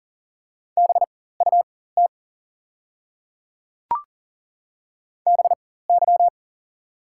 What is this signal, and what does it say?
Morse code, heard as a keyed tone, about 40 wpm, spaced as for 19 wpm, 700 Hz. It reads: BUT BY
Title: 4x Character Spacing - Top 200 Words - 40wpm